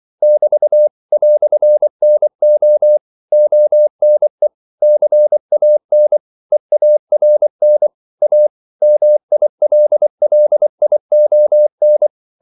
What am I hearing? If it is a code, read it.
Morse code, heard as a keyed tone, about 24 wpm, 600 Hz. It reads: <BT> "NO ONE CAN EARN A MILLION
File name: CW / Practice / EnglishQuotations0324